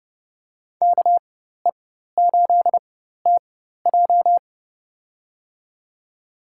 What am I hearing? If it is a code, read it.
Morse code, heard as a keyed tone, about 31 wpm, spaced as for 14 wpm, 700 Hz. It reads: KE8TJ
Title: USA Call Signs - 5 Characters - 4x Character Spacing - 30wpm